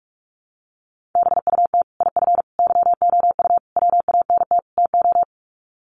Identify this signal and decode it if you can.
Morse code, heard as a keyed tone, about 44 wpm, 700 Hz. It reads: BUT IF YOU WANT TO
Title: Sets of 5 Words Encore2 45wpm